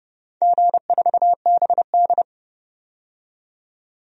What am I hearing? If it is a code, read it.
Morse code, heard as a keyed tone, about 32 wpm, 700 Hz. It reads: G4BD
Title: DXCC-100 Call Signs - 4 Characters - 30wpm